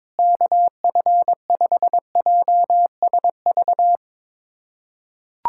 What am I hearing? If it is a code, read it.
Morse code, heard as a keyed tone, about 21 wpm, 700 Hz. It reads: KF5JSV
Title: USA Callsign DE Callsign - 6 Characters - 22wpm